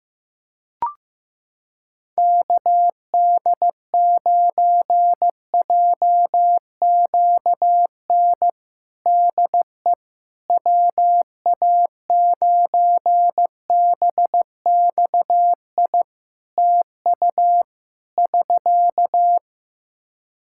Morse code, heard as a keyed tone, about 15 wpm, 700 Hz. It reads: KD9JQN DE WA9BXI TU <SK>
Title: Practice QSO - Exchange 4 - Ending - 15wpm